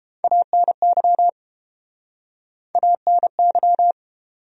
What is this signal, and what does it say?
Morse code, heard as a keyed tone, about 30 wpm, 700 Hz. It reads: ANY ANY